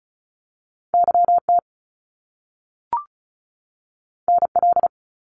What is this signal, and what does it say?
Morse code, heard as a keyed tone, about 36 wpm, 700 Hz. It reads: YT NL
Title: Canadian Provinces and Territories 35wpm